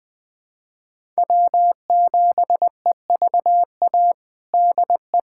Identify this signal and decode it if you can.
Morse code, heard as a keyed tone, about 21 wpm, 700 Hz. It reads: W7EVA DE
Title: USA Callsign DE Callsign 20wpm